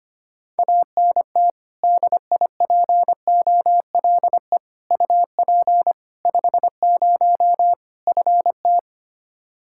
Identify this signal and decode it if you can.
Morse code, heard as a keyed tone, about 25 wpm, 700 Hz. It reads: ANT DIPOLE UP 50 FT
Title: Practice QSO - Exchange 2 - Info - 25wpm